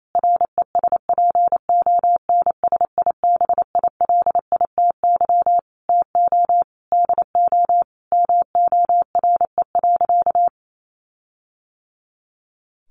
Morse code, heard as a keyed tone, about 28 wpm, 700 Hz. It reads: RESPONSIBILITY TO DO MORE.
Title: Sentences from Top 1400 Words 28wpm